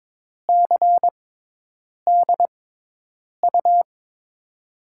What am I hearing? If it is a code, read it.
Morse code, heard as a keyed tone, about 22 wpm, spaced as for 8 wpm, 700 Hz. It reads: CDU
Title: USA Call Signs - 6 Characters - 6x Character Spacing - 22wpm